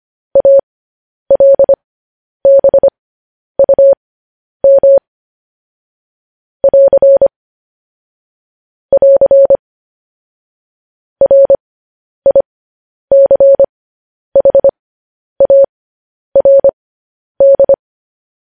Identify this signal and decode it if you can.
Morse code, heard as a keyed tone, about 25 wpm, spaced as for 10 wpm, 550 Hz. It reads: ALBUM <AR> <AR> RICHARD
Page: news-headlines-in-morse-code-at-10-wpm